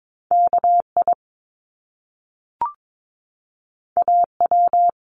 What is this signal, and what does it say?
Morse code, heard as a keyed tone, about 22 wpm, 700 Hz. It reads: KI AW
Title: Top Two Letter Combinations in Words 22wpm